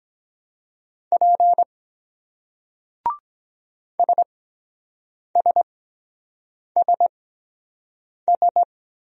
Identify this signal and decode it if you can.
Morse code, heard as a keyed tone, about 23 wpm, spaced as for 7 wpm, 700 Hz. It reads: P SSSS